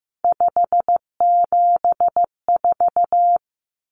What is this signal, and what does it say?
Morse code, heard as a keyed tone, about 15 wpm, 700 Hz. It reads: 574
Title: Common Cut Numbers - 3-Digit 15wpm